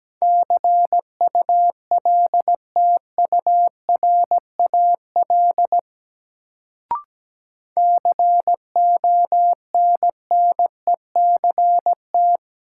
Morse code, heard as a keyed tone, about 17 wpm, 700 Hz. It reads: CULTURAL CONNECT
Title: Sentences from Top 1500 Words - New Word Drill - 17wpm